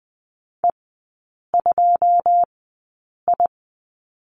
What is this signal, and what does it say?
Morse code, heard as a keyed tone, about 19 wpm, spaced as for 8 wpm, 690 Hz. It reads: E2I
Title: 166 Lesson 27e - Number 7 - Three Characters 20wpm